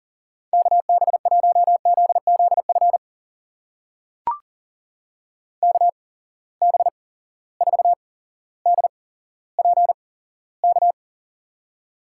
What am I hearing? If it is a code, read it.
Morse code, heard as a keyed tone, about 40 wpm, spaced as for 17 wpm, 700 Hz. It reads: KB1ZZF KB4DPK
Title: USA Call Signs - 6 Characters - 8x Character Spacing - 40wpm